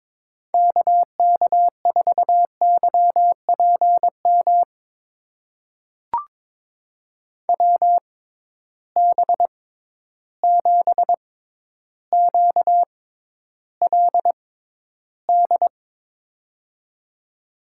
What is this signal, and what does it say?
Morse code, heard as a keyed tone, about 22 wpm, spaced as for 11 wpm, 700 Hz. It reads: KK4YPM WB7QLD
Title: USA Call Signs - 6 Characters - 6x Character Spacing - 22wpm